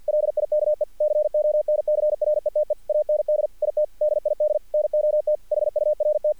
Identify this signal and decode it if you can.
Morse code, heard as a keyed tone, about 50 wpm, 600 Hz, with radio noise. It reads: LIKE CONCRETE AND IT DID NOT HURT
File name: archived / CRSnet